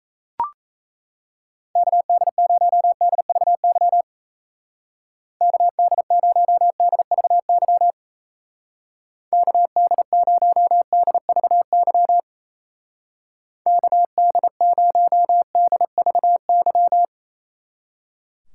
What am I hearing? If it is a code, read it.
Morse code, heard as a keyed tone, about 34 wpm, 700 Hz. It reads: KD0DVY KD0DVY KD0DVY KD0DVY
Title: Speed Racing - USA 2x3 Call Signs 28wpm